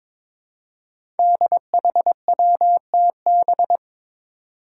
Morse code, heard as a keyed tone, about 22 wpm, 700 Hz. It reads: DHWTB
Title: Five Challenging Letter-Number Combinations 22wpm